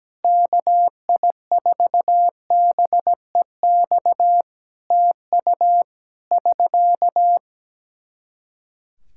Morse code, heard as a keyed tone, about 17 wpm, 700 Hz. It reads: KI4BEX TU <SK>
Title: K1USN SST QSO Practice 17wpm